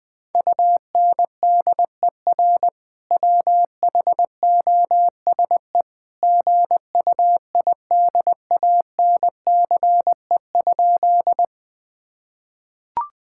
Morse code, heard as a keyed tone, about 20 wpm, 700 Hz. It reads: UNDER WHOSE GUIDANCE?
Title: Sentences from Top 1600 Words 20wpm